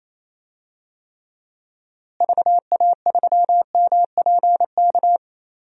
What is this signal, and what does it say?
Morse code, heard as a keyed tone, about 28 wpm, 700 Hz. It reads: VA3MPK
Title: DXCC-100 Call Signs - 6 Characters - Repeated 2x - 28wpm